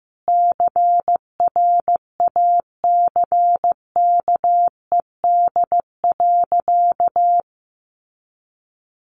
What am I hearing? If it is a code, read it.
Morse code, heard as a keyed tone, about 15 wpm, 700 Hz. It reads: CRACKED.
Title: A Sound Of Thunder 15wpm